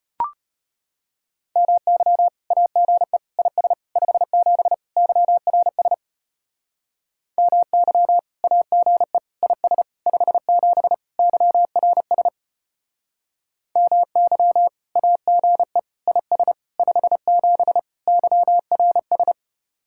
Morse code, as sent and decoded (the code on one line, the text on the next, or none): -- -.-- / .- --. / .. ... / ..... --... / -.-- .-. ... / -- -.-- / .- --. / .. ... / ..... --... / -.-- .-. ... / -- -.-- / .- --. / .. ... / ..... --... / -.-- .-. ...
MY AGE IS 57 YRS MY AGE IS 57 YRS MY AGE IS 57 YRS